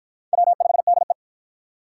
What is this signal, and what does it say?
Morse code, heard as a keyed tone, about 53 wpm, 700 Hz. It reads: W5DE